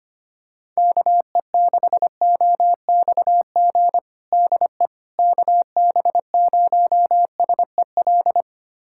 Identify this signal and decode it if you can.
Morse code, heard as a keyed tone, about 25 wpm, 700 Hz. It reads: KE6OXG DE KB0SEL
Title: USA Callsign DE Callsign - 6 Characters - 25wpm